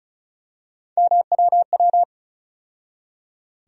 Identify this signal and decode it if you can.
Morse code, heard as a keyed tone, about 36 wpm, 700 Hz. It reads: MWW
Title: USA Call Sign Postfixes - 3 Letters 35wpm